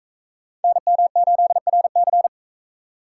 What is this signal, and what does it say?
Morse code, heard as a keyed tone, about 42 wpm, 700 Hz. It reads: NM8RC